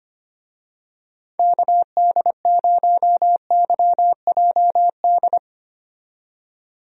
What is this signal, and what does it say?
Morse code, heard as a keyed tone, about 25 wpm, 700 Hz. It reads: KD0YJD
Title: USA Call Signs - 6 Characters - 25wpm